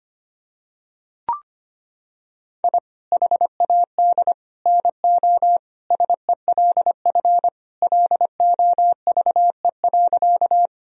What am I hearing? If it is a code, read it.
Morse code, heard as a keyed tone, about 25 wpm, 700 Hz. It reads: I HAD NO SELF LOVE.
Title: Sentences from Top 1000 Words - Review Part 1 - 25wpm